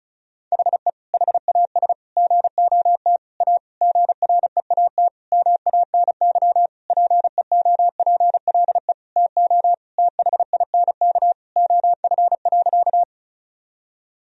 Morse code, heard as a keyed tone, about 35 wpm, 700 Hz. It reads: HE HAS GOT A GREAT MANY PEOPLE TO THINK OF.
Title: Sentences from Top 1000 Words - Review Part 1 - 35wpm